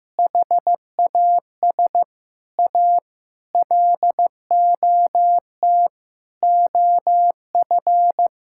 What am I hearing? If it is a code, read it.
Morse code, heard as a keyed tone, about 15 wpm, 700 Hz. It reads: HAS A LOT OF
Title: Sentences from Top 1000 Words - Review Part 5 - 15wpm